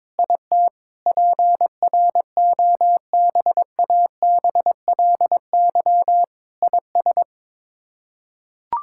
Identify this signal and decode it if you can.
Morse code, heard as a keyed tone, about 22 wpm, 700 Hz. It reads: IT PROBABLY IS
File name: Sets of 9 Words - 22wpm